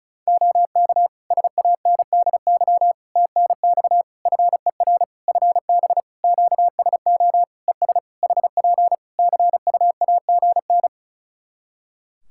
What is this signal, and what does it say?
Morse code, heard as a keyed tone, about 36 wpm, 700 Hz. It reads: OK SANDY TNX FER FB QSO ES HP CUAGN
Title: K1USN SST QSO Practice 35wpm